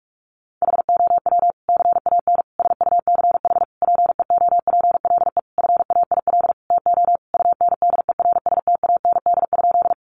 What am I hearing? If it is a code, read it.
Morse code, heard as a keyed tone, about 44 wpm, 700 Hz. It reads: HOW CAN SUCH PEOPLE FAIL TO UNDERSTAND?
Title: Sentences from Top 1100 Words 45wpm